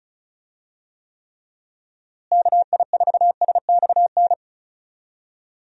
Morse code, heard as a keyed tone, about 36 wpm, 700 Hz. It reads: KI4SXN
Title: USA Call Signs - 6 Characters - Repeated 3x - 35wpm